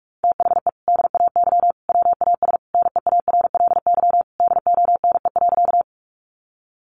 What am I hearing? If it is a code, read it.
Morse code, heard as a keyed tone, about 44 wpm, 700 Hz. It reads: THE DAY WAS NEARLY DONE.
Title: Sentences from Top 1300 Words 45wpm